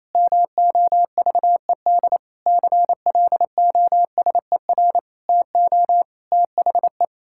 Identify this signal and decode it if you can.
Morse code, heard as a keyed tone, about 28 wpm, 700 Hz. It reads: MOVED CLOSER TO THE
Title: Pangram Challenge v2 - 28wpm